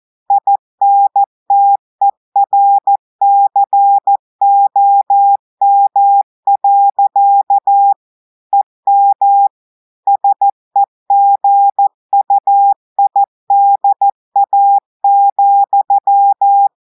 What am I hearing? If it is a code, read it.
Morse code, heard as a keyed tone, about 14 wpm, 800 Hz. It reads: INTERCOM. EM SEGUIDA,
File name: audios / portugais / portugais texte en clair 1 vitesse 14 mots minutes